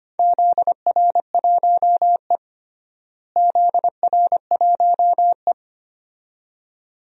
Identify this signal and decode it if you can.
Morse code, heard as a keyed tone, about 25 wpm, 700 Hz. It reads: ZR1E ZR1E
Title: DXCC-100 Call Signs - 4 Characters - Repeated 2x - 25wpm